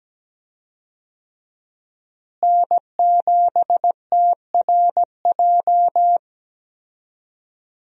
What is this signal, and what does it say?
Morse code, heard as a keyed tone, about 17 wpm, 700 Hz. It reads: N7TRJ
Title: USA Call Signs - 5 Characters - Repeated 3x - 17wpm